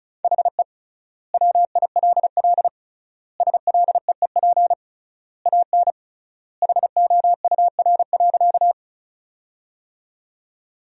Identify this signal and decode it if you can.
Morse code, heard as a keyed tone, about 34 wpm, 700 Hz. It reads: HE WILL SLEEP AN HOUR.
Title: 3x Word-Spacing - Sentences from Top 500 Words - 35wpm